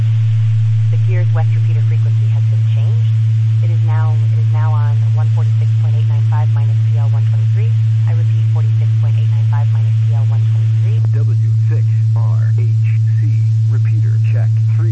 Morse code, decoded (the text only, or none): none